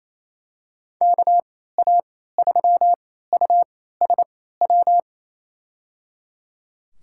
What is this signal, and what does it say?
Morse code, heard as a keyed tone, about 28 wpm, spaced as for 16 wpm, 700 Hz. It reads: KA3USW
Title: USA Call Signs - 6 Characters - 3x Character Spacing - 28wpm